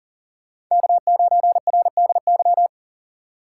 Morse code, heard as a keyed tone, about 40 wpm, 700 Hz. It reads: K9RDY
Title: USA Call Signs - 5 Characters - 6x Character Spacing - 40wpm